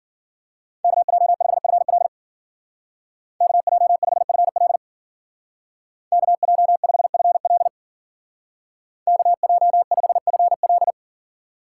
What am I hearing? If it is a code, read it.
Morse code, heard as a keyed tone, about 52 wpm, 700 Hz. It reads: KJ5FL KJ5FL KJ5FL KJ5FL